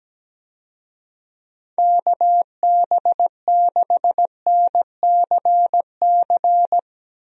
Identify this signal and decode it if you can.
Morse code, heard as a keyed tone, about 17 wpm, 700 Hz. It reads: KB6NCC
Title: USA Call Signs - 6 Characters - Repeated 2x - 17wpm